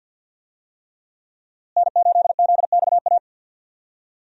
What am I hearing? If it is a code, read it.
Morse code, heard as a keyed tone, about 47 wpm, 700 Hz. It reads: N8ZXA
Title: USA Call Signs - 5 Characters - Repeated 3x - 50wpm